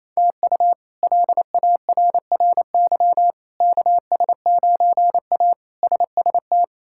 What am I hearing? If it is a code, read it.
Morse code, heard as a keyed tone, about 28 wpm, 700 Hz. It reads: TU LARRY KS9A SST
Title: K1USN SST QSO Practice - 28wpm